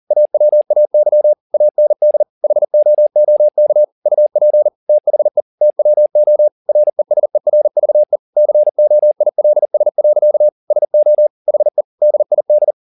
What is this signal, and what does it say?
Morse code, heard as a keyed tone, about 40 wpm, 600 Hz. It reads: AWAY AND HOOK UP THE TWO RESERVE COILS. SO HE DID